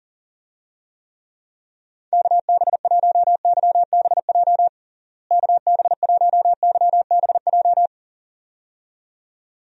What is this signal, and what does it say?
Morse code, heard as a keyed tone, about 40 wpm, 700 Hz. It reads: KB1YBJ KB1YBJ
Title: USA Call Signs - 6 Characters - Repeated 2x - 40wpm